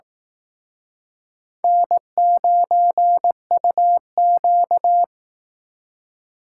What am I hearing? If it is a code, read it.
Morse code, heard as a keyed tone, about 18 wpm, 700 Hz. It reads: N9UQ
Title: Speed Racing - USA 1x2 Call Signs 15wpm